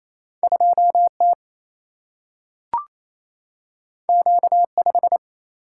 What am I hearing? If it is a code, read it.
Morse code, heard as a keyed tone, about 28 wpm, 700 Hz. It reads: 2 T Q 5
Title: Single-Letter Single-Number 28wpm